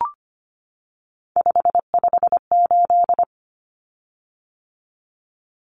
Morse code, heard as a keyed tone, about 25 wpm, 700 Hz. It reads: 558
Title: Common Cut Numbers - 3-Digit 25wpm